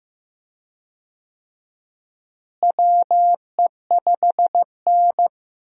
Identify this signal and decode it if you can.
Morse code, heard as a keyed tone, about 15 wpm, 700 Hz. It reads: WE5N
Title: USA Call Signs - 4 Characters - Repeated 2x - 15wpm